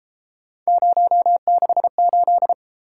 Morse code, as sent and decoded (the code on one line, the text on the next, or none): ----- -.... ---..
068